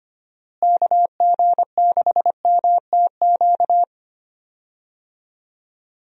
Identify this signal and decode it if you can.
Morse code, heard as a keyed tone, about 25 wpm, 700 Hz. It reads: KG6MTQ
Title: USA Call Signs - 6 Characters - Repeated 3x - 25wpm